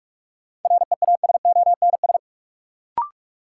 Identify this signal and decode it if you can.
Morse code, heard as a keyed tone, about 46 wpm, 700 Hz. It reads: REASONS
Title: Sets of 4 Words - New Word Drill - 45wpm